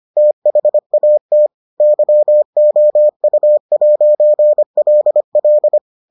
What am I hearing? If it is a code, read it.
Morse code, heard as a keyed tone, about 25 wpm, 600 Hz. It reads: THAT YOU'LL